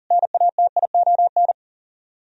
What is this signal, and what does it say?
Morse code, heard as a keyed tone, about 52 wpm, 700 Hz. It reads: NATION